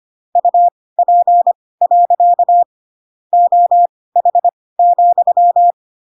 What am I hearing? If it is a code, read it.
Morse code, heard as a keyed tone, about 25 wpm, spaced as for 18 wpm, 700 Hz. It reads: UP. OH,
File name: Pinocchio / Pinocchio ch7 12 25 ik1hge